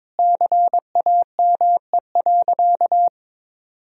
Morse code, heard as a keyed tone, about 22 wpm, 700 Hz. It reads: CAME.
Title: Sentences from Top 1000 Words - Review Part 2 - 22wpm